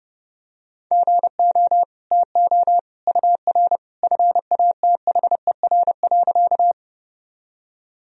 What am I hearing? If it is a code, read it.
Morse code, heard as a keyed tone, about 29 wpm, 700 Hz. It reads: GO TO UR FATHER.
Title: Sentences from Top 300 Words and Abbreviations 30wpm